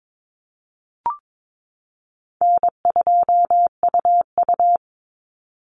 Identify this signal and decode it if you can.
Morse code, heard as a keyed tone, about 22 wpm, 700 Hz. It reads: N2UU
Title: Easy Call Signs Repeated 3x - 22wpm